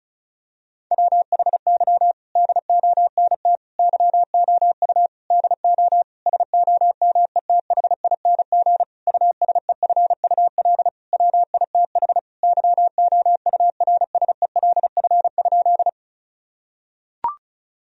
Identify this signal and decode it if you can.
Morse code, heard as a keyed tone, about 36 wpm, 700 Hz. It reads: WHY DONT YOU DO SOMETHING USEFUL WITH YOURSELF?
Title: Sentences from Top 1000 Words - Review Part 4 - 35wpm